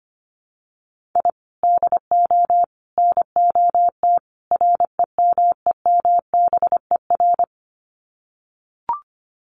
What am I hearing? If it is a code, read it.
Morse code, heard as a keyed tone, about 25 wpm, 700 Hz. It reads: I DO NOT REMEMBER
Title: Sets of 4 Words 25wpm